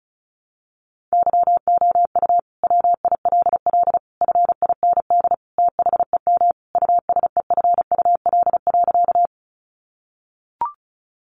Morse code, heard as a keyed tone, about 34 wpm, 700 Hz. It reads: YOU WILL FIND THEM USEFUL.
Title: Sentences from Top 700 Words 35wpm